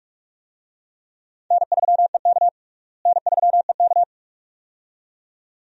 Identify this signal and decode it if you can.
Morse code, heard as a keyed tone, about 46 wpm, 700 Hz. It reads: N3EK N3EK
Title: USA Call Signs - 4 Characters - Repeated 2x - 45wpm